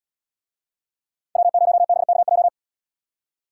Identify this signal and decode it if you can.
Morse code, heard as a keyed tone, about 77 wpm, 700 Hz. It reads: W0BKJ